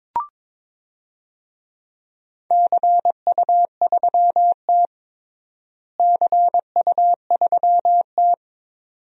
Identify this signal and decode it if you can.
Morse code, heard as a keyed tone, about 22 wpm, 700 Hz. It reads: CU3T CU3T
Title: DXCC-100 Call Signs - 4 Characters - Repeated 2x - 22wpm